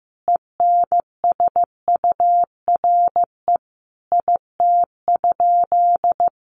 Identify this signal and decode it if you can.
Morse code, heard as a keyed tone, about 14 wpm, 700 Hz. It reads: ENSURE IT?
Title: Sentences from Top 1100 Words 15wpm